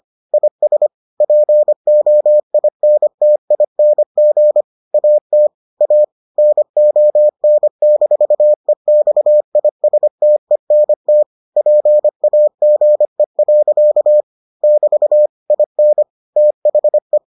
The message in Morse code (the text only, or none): IS POINTING AT A NON-EXISTENT PAGE. <BT> IN THE